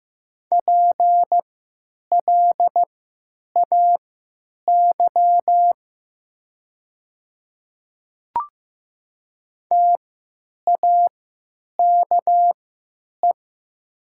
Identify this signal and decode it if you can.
Morse code, heard as a keyed tone, about 15 wpm, spaced as for 9 wpm, 700 Hz. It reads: PLAY TAKE
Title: 3x Character Spacing - Top 300 Words - 15wpm